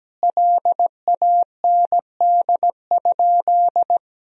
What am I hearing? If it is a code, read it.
Morse code, heard as a keyed tone, about 17 wpm, 700 Hz. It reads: LAND?
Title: Sentences from Top 1000 Words - Review Part 4 - 17wpm